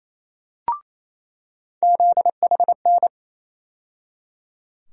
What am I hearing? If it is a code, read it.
Morse code, heard as a keyed tone, about 28 wpm, 700 Hz. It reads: ZHN
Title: USA Call Sign Postfixes - 3 Letters 28wpm